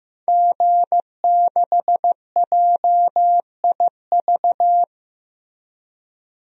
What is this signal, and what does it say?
Morse code, heard as a keyed tone, about 15 wpm, 700 Hz. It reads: G6JIV
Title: DXCC-100 Call Signs - 5 Characters - Repeated 3x - 15wpm